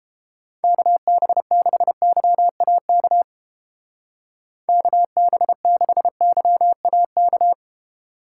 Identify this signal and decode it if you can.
Morse code, heard as a keyed tone, about 29 wpm, 700 Hz. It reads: KB6YAK KB6YAK